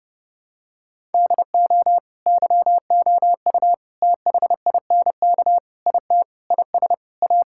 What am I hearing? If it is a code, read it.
Morse code, heard as a keyed tone, about 30 wpm, 700 Hz. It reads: DO YOU THINK IT IS A
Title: Sentences from Top 1000 Words - Review Part 6 - 30wpm